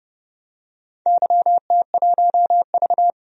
Morse code, heard as a keyed tone, about 30 wpm, 700 Hz. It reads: YT1V
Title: DXCC-100 Call Signs - 4 Characters - 30wpm